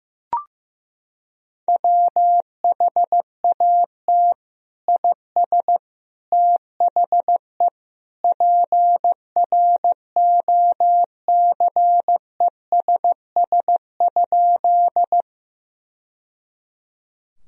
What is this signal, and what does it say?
Morse code, heard as a keyed tone, about 15 wpm, 700 Hz. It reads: WHAT IS THE PROCESS?
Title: Sentences from Top 1000 Words - Review Part 3 - 15wpm